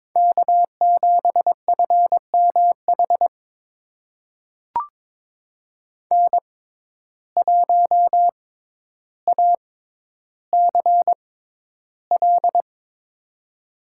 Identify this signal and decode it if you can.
Morse code, heard as a keyed tone, about 22 wpm, spaced as for 11 wpm, 700 Hz. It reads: K7FMH N1ACL
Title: USA Call Signs - 5 Characters - 6x Character Spacing - 22wpm